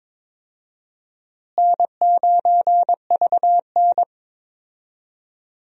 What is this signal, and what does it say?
Morse code, heard as a keyed tone, about 22 wpm, 700 Hz. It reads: N9VN
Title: USA Call Signs - 4 Characters - Repeated 3x - 22wpm